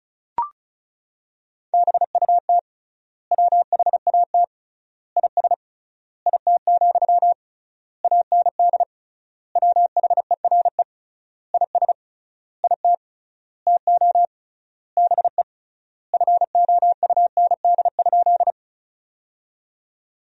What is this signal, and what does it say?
Morse code, heard as a keyed tone, about 36 wpm, 700 Hz. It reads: BUT WHAT IS IT, AND WHERE IS IT TO BE FOUND?
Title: 3x Word-Spacing - Sentences from Top 300 Words - 35wpm